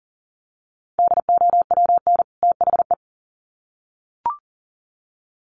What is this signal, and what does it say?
Morse code, heard as a keyed tone, about 40 wpm, 700 Hz. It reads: DOWN THE
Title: Sets of 2 Words 40wpm